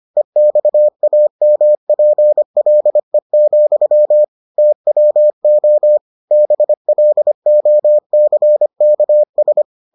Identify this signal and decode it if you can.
Morse code, heard as a keyed tone, about 25 wpm, 600 Hz. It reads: EXAMPLE, TWO BLOCKS